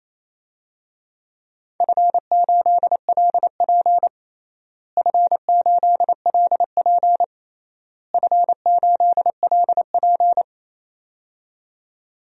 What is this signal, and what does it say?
Morse code, heard as a keyed tone, about 28 wpm, 700 Hz. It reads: F8LP F8LP F8LP
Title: DXCC-100 Call Signs - 4 Characters - Repeated 3x - 28wpm